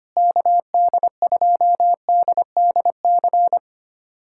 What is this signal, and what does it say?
Morse code, heard as a keyed tone, about 25 wpm, 700 Hz. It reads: KD2DDC
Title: USA Callsign DE Callsign 25wpm